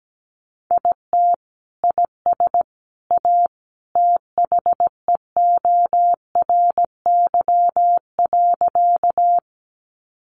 Morse code, heard as a keyed tone, about 17 wpm, 700 Hz. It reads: IT IS A THEORY.
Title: Sentences from Top 600 Words 17wpm